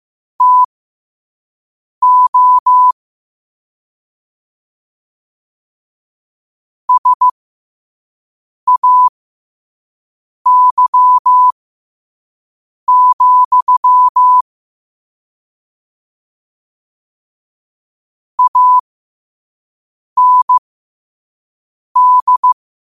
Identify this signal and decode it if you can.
Morse code, heard as a keyed tone, about 15 wpm, spaced as for 5 wpm, 1000 Hz. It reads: TO SAY, AND
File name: qotdfiles / 5wpm